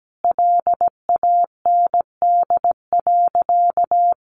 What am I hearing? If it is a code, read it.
Morse code, heard as a keyed tone, about 17 wpm, 700 Hz. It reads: LAND.
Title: Sentences from Top 1200 Words 17wpm